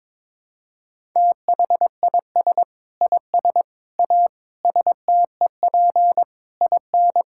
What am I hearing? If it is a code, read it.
Morse code, heard as a keyed tone, about 22 wpm, 700 Hz. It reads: THIS IS A STEP IN
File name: practice / 22wpm / Sets of 8 Words - 22wpm